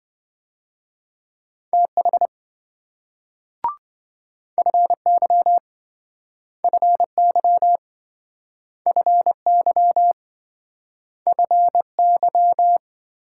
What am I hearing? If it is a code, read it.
Morse code, heard as a keyed tone, about 26 wpm, 700 Hz. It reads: TH FY FY FY FY